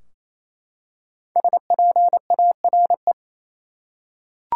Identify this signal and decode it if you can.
Morse code, heard as a keyed tone, about 28 wpm, 700 Hz. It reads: SPARE
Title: Five Letter Words - 28wpm